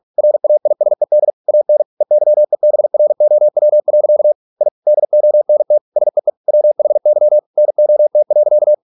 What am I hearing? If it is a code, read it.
Morse code, heard as a keyed tone, about 46 wpm, 600 Hz. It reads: RAISED AN EYEBROW. I DONT SEE WHY NOT.